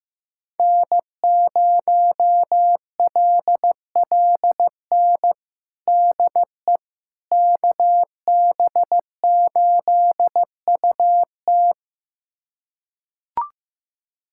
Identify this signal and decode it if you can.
Morse code, heard as a keyed tone, about 15 wpm, 700 Hz. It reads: N0LLN DE KB8UT
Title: USA Callsign DE Callsign - 5 Characters - 15wpm